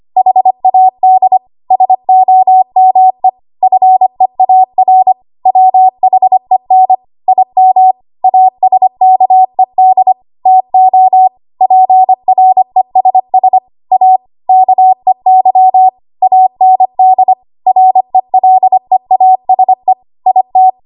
Morse code, heard as a keyed tone, about 25 wpm, 750 Hz. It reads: HAD SOME FEAR WHEN IM ASKED TO PRESS A KEY AND RELEASE IT
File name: Archive / 25 WPM / 25WPM